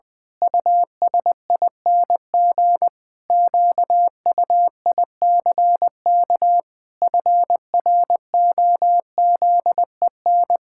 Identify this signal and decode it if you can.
Morse code, heard as a keyed tone, about 20 wpm, 700 Hz. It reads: USING QUICK FROZEN
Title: Pangram Challenge v2 - 20wpm